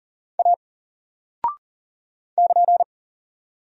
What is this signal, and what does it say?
Morse code, heard as a keyed tone, about 40 wpm, 700 Hz. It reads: A <KN>
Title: Single Character (Letter:Number:Punctuation:Prosign) 40wpm